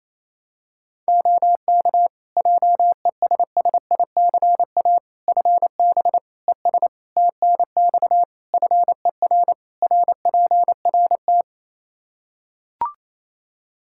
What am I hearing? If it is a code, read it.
Morse code, heard as a keyed tone, about 28 wpm, 700 Hz. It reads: OK JESSICA FB ES TNX FER RPRT
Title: Practice QSO - Exchange 2 - Info - 28wpm